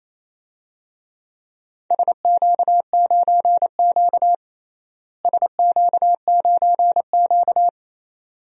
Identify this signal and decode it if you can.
Morse code, heard as a keyed tone, about 28 wpm, 700 Hz. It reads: SQ9Q SQ9Q
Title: DXCC-100 Call Signs - 4 Characters - Repeated 2x - 28wpm